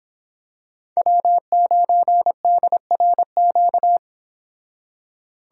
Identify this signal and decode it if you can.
Morse code, heard as a keyed tone, about 26 wpm, 700 Hz. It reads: W9DRQ
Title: Speed Racing - USA 1x3 Call Signs 22wpm